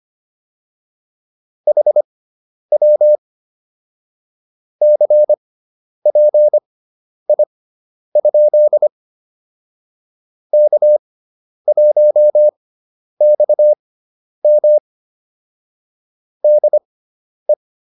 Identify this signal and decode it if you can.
Morse code, heard as a keyed tone, about 25 wpm, spaced as for 10 wpm, 600 Hz. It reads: HW CPI? K1XM DE